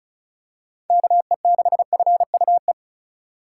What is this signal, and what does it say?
Morse code, heard as a keyed tone, about 36 wpm, 690 Hz. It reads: KE6FUE
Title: USA Call Signs - 6 Characters - 2x Character Spacing - 35wpm